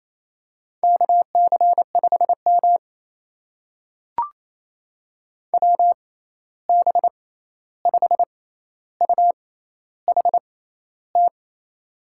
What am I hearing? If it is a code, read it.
Morse code, heard as a keyed tone, about 28 wpm, spaced as for 13 wpm, 700 Hz. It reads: KC5M WB5UHT
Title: USA Call Signs - 6x Character Spacing - 28wpm